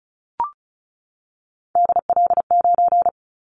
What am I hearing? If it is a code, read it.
Morse code, heard as a keyed tone, about 34 wpm, 700 Hz. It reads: DL9
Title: Callsign Prefixes for Top 50 DXCC - 35wpm